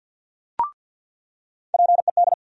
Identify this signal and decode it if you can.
Morse code, heard as a keyed tone, about 47 wpm, 700 Hz. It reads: PED
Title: USA Call Sign Postfixes - 3 Letters 50wpm